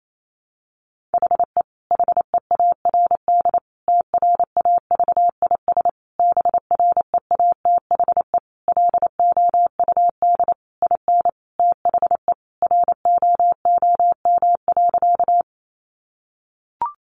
Morse code, heard as a keyed tone, about 28 wpm, 700 Hz. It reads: HE HEARD TRAVIS BREATHE LOUD IN THE ROOM.
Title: A Sound Of Thunder 28wpm